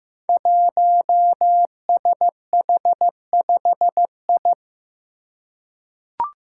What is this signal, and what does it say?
Morse code, heard as a keyed tone, about 15 wpm, 700 Hz. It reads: 1SH5I
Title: Five Challenging Letter-Number Combinations 15wpm